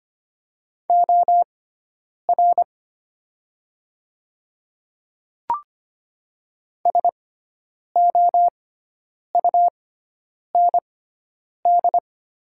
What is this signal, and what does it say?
Morse code, heard as a keyed tone, about 25 wpm, spaced as for 9 wpm, 700 Hz. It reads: OR SOUND
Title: 6x Character Spacing - Top 100 Words - 25wpm